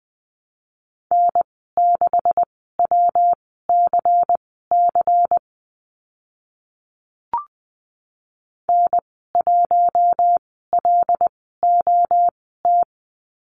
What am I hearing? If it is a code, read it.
Morse code, heard as a keyed tone, about 20 wpm, spaced as for 14 wpm, 700 Hz. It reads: N6WCC N1LOT
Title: 186 Lesson 30g - Letter b - Call Signs 20wpm